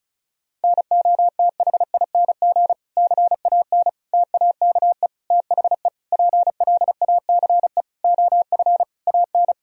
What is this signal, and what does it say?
Morse code, heard as a keyed tone, about 34 wpm, 700 Hz. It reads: NOTHING CAN TAKE THE PLACE OF AN
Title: Sentences from Top 1000 Words - Review Part 4 - 35wpm